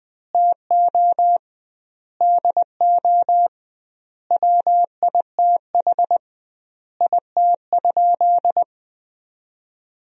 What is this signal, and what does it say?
Morse code, heard as a keyed tone, about 21 wpm, 700 Hz. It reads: TO DO WITH IT?
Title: 2x Word-Spacing - Sentences from Top 100 Words - 20wpm